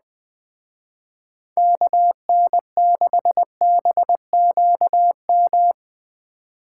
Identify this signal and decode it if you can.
Morse code, heard as a keyed tone, about 21 wpm, 700 Hz. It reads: KN6BQM